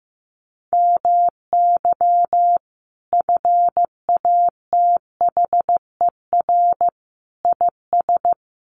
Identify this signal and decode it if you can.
Morse code, heard as a keyed tone, about 15 wpm, 700 Hz. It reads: MY FATHER IS
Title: Sentences from Top 300 Words 15wpm